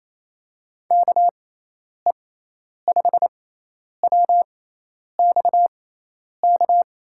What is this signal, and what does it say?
Morse code, heard as a keyed tone, about 28 wpm, spaced as for 10 wpm, 700 Hz. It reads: KE5WXK
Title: USA Call Signs - 6 Characters - 6x Character Spacing - 28wpm